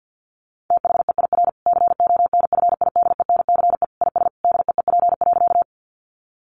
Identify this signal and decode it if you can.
Morse code, heard as a keyed tone, about 49 wpm, 700 Hz. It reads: THEIR CONFIDENCE IS DEEP.
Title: Sentences from Top 1500 Words 50wpm